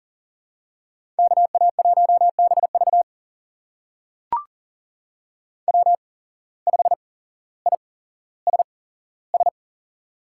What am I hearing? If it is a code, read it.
Morse code, heard as a keyed tone, about 40 wpm, spaced as for 17 wpm, 700 Hz. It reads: KA1BV W5ISS
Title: USA Call Signs - 8x Character Spacing - 40wpm